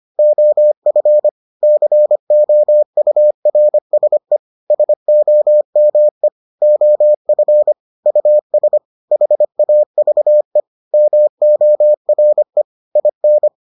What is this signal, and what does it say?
Morse code, heard as a keyed tone, about 25 wpm, 600 Hz. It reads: OF COURSE SOME OF US HAVE MORE IN